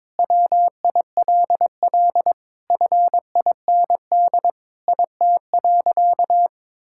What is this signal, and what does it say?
Morse code, heard as a keyed tone, about 22 wpm, 700 Hz. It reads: WILL FIND IT.
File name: practice / 22wpm / Sentences with 3-Word Phrases - 22wpm